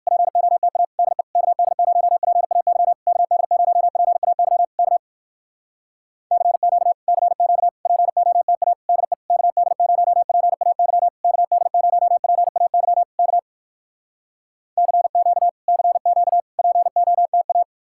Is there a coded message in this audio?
Yes: Morse code from 0.1 to 17.6 s, heard as a keyed tone, about 59 wpm, 700 Hz. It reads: POTA DE KD0PAY KD0PAY K CQ CQ POTA DE KD0PAY KD0PAY K CQ CQ POTA